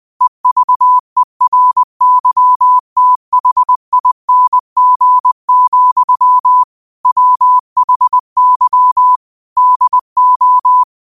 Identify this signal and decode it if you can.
Morse code, heard as a keyed tone, about 20 wpm, 1000 Hz. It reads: EVERYTHING, WHY DO
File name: 20wpm